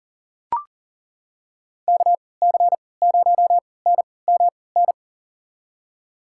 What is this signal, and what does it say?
Morse code, heard as a keyed tone, about 37 wpm, spaced as for 22 wpm, 700 Hz. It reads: KC0NMN
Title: USA Call Signs - 3x Character Spacing - 40wpm